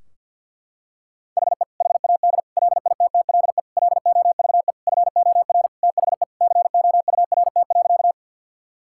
Morse code, heard as a keyed tone, about 50 wpm, 700 Hz. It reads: HE HAD LITTLE LOVE FOR THE COURT.
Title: Sentences from Top 1600 Words 50wpm